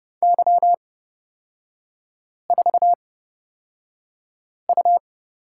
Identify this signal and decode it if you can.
Morse code, heard as a keyed tone, about 30 wpm, spaced as for 5 wpm, 690 Hz. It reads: Y4U
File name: Lesson 37e - Letter x - Three Characters 30wpm